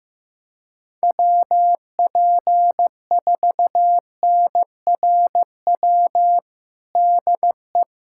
Morse code, heard as a keyed tone, about 15 wpm, 700 Hz. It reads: WP4NRW DE
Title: USA Callsign DE Callsign - 6 Characters - 15wpm